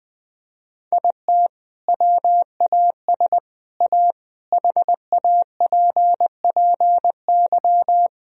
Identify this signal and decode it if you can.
Morse code, heard as a keyed tone, about 20 wpm, 700 Hz. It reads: IT WAS A HAPPY
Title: Sentences from Top 1000 Words - Review Part 1 - 20wpm